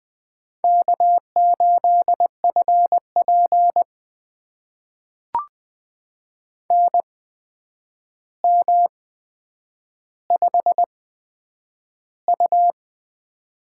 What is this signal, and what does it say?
Morse code, heard as a keyed tone, about 21 wpm, spaced as for 9 wpm, 700 Hz. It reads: K8FP NM5U
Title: USA Call Signs - 4 Characters - 8x Character Spacing - 20wpm